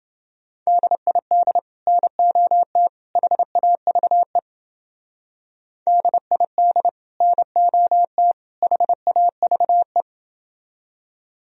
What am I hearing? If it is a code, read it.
Morse code, heard as a keyed tone, about 28 wpm, 700 Hz. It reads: DID NOT HAVE DID NOT HAVE